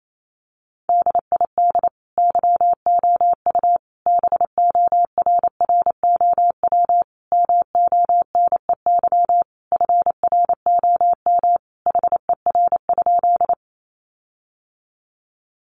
Morse code, heard as a keyed tone, about 28 wpm, 700 Hz. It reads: DID YOU BORROW MONEY FROM HER?
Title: Sentences from Top 1500 Words 28wpm